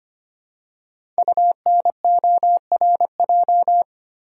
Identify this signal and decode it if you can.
Morse code, heard as a keyed tone, about 25 wpm, 700 Hz. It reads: UNORJ
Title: Code Groups - 5 Letters - 25wpm